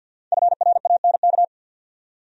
Morse code, heard as a keyed tone, about 51 wpm, 700 Hz. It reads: FRANK
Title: USA Common Names - 50wpm